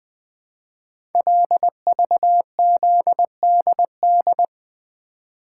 Morse code, heard as a keyed tone, about 20 wpm, 700 Hz. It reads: LVZDD
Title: Code Groups - 5 Letters - 20wpm